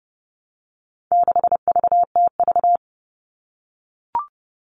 Morse code, heard as a keyed tone, about 30 wpm, 700 Hz. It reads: 6VTV
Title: All Cut Numbers - 4-Digit 30wpm